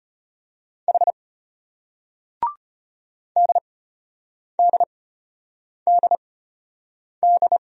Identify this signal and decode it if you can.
Morse code, heard as a keyed tone, about 35 wpm, spaced as for 8 wpm, 700 Hz. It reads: H DDDD